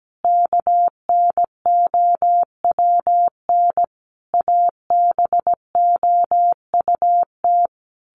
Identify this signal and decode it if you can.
Morse code, heard as a keyed tone, about 17 wpm, 700 Hz. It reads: KNOWN ABOUT
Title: Sentences from Top 1200 Words 17wpm